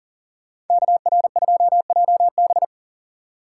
Morse code, heard as a keyed tone, about 49 wpm, 700 Hz. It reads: KR2JB
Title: Speed Racing - USA 2x2 Call Signs 40wpm